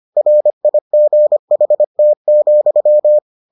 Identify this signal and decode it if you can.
Morse code, heard as a keyed tone, about 25 wpm, 600 Hz. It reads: RIGHT,